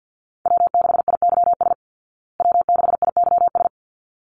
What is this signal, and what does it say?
Morse code, heard as a keyed tone, about 50 wpm, 700 Hz. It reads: W6IYS W6IYS
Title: Call Signs Repeated 3x - 50wpm